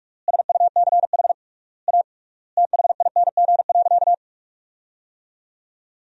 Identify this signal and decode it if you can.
Morse code, heard as a keyed tone, about 41 wpm, 700 Hz. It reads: SUCH A THING.
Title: 3x Word-Spacing - Sentences from Top 200 Words - 45wpm